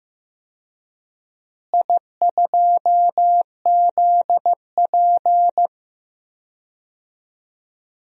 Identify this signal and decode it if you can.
Morse code, heard as a keyed tone, about 15 wpm, 700 Hz. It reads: I2ZP
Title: DXCC-100 Call Signs - 4 Characters - Repeated 3x - 15wpm